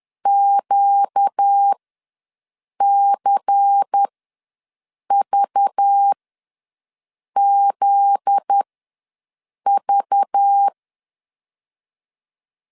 Morse code, heard as a keyed tone, about 10 wpm, spaced as for 6 wpm, 800 Hz. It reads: QCVZV
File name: LEZIONI / 7 GF9Z / 7 GF9Z ws 10wpm 3919